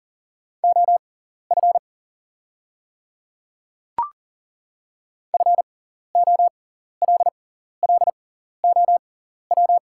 Morse code, heard as a keyed tone, about 37 wpm, spaced as for 14 wpm, 700 Hz. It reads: OF FOLLOW
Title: 6x Character Spacing - Top 300 Words - 40wpm